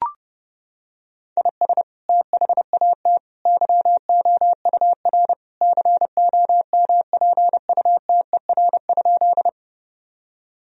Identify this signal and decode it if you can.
Morse code, heard as a keyed tone, about 32 wpm, 700 Hz. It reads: IS THAT YOUR COMPUTER?
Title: Sentences from Top 1000 Words - Review Part 3 - 30wpm